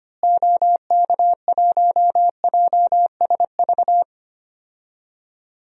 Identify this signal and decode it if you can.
Morse code, heard as a keyed tone, about 25 wpm, 700 Hz. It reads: OK1JSV
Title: DXCC-100 Call Signs - 6 Characters - Repeated 2x - 25wpm